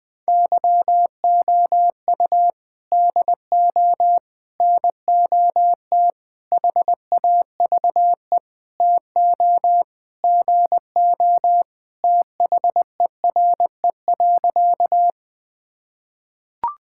Morse code, heard as a keyed tone, about 20 wpm, 700 Hz. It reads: YOU DO NOT HAVE TO GO THERE.
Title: Sentences with 5-Word Phrases - 20wpm